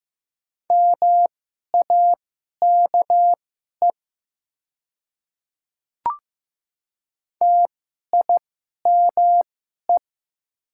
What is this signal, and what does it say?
Morse code, heard as a keyed tone, about 15 wpm, spaced as for 11 wpm, 700 Hz. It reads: MAKE TIME
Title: 2x Character Spacing - Top 200 Words - 15wpm